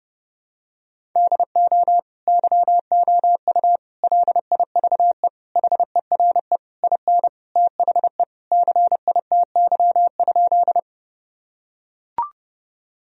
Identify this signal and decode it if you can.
Morse code, heard as a keyed tone, about 28 wpm, 700 Hz. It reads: DO YOU LIVE HERE IN THE CITY?
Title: Sentences from Top 1000 Words - Review Part 1 - 30wpm